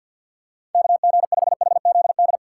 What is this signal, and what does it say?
Morse code, heard as a keyed tone, about 46 wpm, 700 Hz. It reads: KG5HZD
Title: USA Call Signs - 6 Characters - 3x Character Spacing - 50wpm